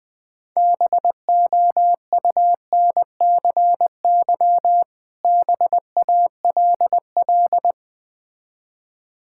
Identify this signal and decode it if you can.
Morse code, heard as a keyed tone, about 20 wpm, 700 Hz. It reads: BOUNCY BALL
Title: Difficult Words with Hints - 20wpm